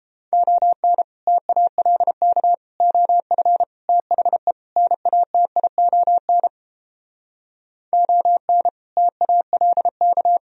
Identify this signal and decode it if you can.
Morse code, heard as a keyed tone, about 29 wpm, 700 Hz. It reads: ON TALK OF THE NATION ON TALK